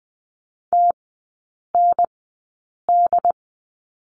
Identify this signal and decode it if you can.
Morse code, heard as a keyed tone, about 20 wpm, spaced as for 8 wpm, 700 Hz. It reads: TND
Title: 081 Lesson 14e - Number 2 - Three Characters 20wpm